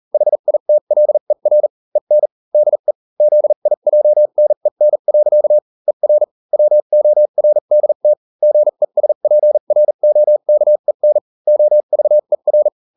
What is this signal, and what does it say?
Morse code, heard as a keyed tone, about 40 wpm, spaced as for 35 wpm, 600 Hz. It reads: HITLER EN DE ZIJNEN. ER WORDT GESPROKEN OVER